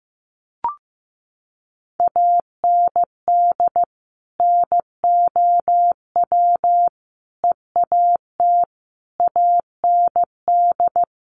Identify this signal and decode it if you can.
Morse code, heard as a keyed tone, about 15 wpm, 700 Hz. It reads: AND NOW EAT AND
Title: Sentences from Top 1500 Words 15wpm